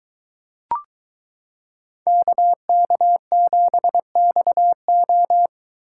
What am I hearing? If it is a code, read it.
Morse code, heard as a keyed tone, about 23 wpm, 700 Hz. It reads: KK7XO